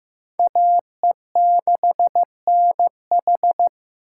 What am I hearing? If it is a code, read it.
Morse code, heard as a keyed tone, about 18 wpm, 700 Hz. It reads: AE6NH
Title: Speed Racing - USA Call Signs 15wpm